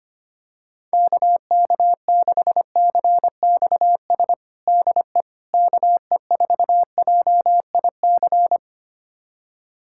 Morse code, heard as a keyed tone, about 25 wpm, 700 Hz. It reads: KK6CXS DE KE4JIC
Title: USA Callsign DE Callsign - 6 Characters - 25wpm